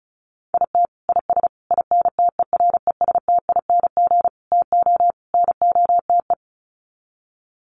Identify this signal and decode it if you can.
Morse code, heard as a keyed tone, about 34 wpm, 700 Hz. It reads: IT IS INTERESTING TO NOTE
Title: Sets of 5 Words Encore2 35wpm